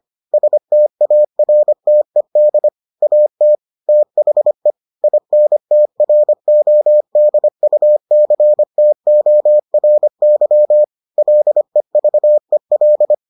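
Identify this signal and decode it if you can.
Morse code, heard as a keyed tone, about 25 wpm, 600 Hz. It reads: STARTED AT THE INTRODUCTORY LEVEL
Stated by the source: Morse edition of the Foundations of Amateur Radio episodes.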